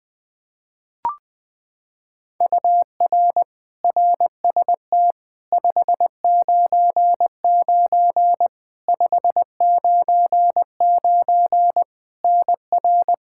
Morse code, heard as a keyed tone, about 20 wpm, 700 Hz. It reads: UR RST 599 599 NR
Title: Practice QSO - Exchange 1 - Beginning - 20wpm